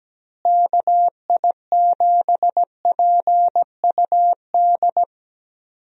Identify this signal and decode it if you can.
Morse code, heard as a keyed tone, about 17 wpm, 700 Hz. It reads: KI7PUD
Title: USA Call Signs - 6 Characters - Repeated 2x - 17wpm